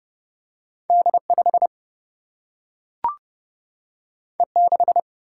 Morse code, heard as a keyed tone, about 30 wpm, 700 Hz. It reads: D5 E6
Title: Single-Letter Single-Number 30wpm